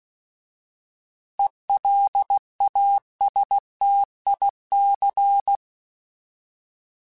Morse code, heard as a keyed tone, about 16 wpm, 800 Hz. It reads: ELASTIC
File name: Alphabetical / Alphabetical mp3s / Es @15